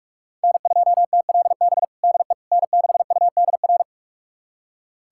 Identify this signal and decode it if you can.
Morse code, heard as a keyed tone, about 42 wpm, 700 Hz. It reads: N2TLB DE N6UDR
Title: USA Callsign DE Callsign - 5 Characters - 45wpm